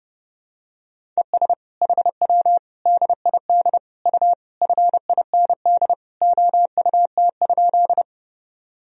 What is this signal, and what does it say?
Morse code, heard as a keyed tone, about 32 wpm, 700 Hz. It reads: ES HW DID U FIND OUT?
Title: Sentences from Top 200 Words and Abbreviations 30wpm